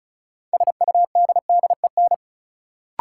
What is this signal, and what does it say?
Morse code, heard as a keyed tone, about 36 wpm, 700 Hz. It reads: SUDDEN
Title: Sets of 4 Words - New Word Drill - 35wpm